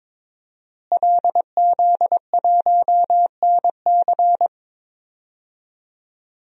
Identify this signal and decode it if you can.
Morse code, heard as a keyed tone, about 21 wpm, 700 Hz. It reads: LZ1NC
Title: DXCC-100 Call Signs - 5 Characters - Repeated 2x - 22wpm